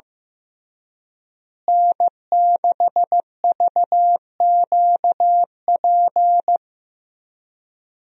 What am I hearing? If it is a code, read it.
Morse code, heard as a keyed tone, about 15 wpm, 700 Hz. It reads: N6VQP
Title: USA Call Signs - 5 Characters - Repeated 3x - 15wpm